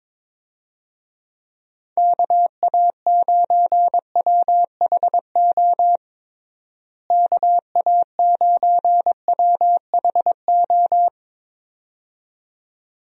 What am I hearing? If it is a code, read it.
Morse code, heard as a keyed tone, about 22 wpm, 700 Hz. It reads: KA9WHO KA9WHO
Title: USA Call Signs - 6 Characters - Repeated 2x - 22wpm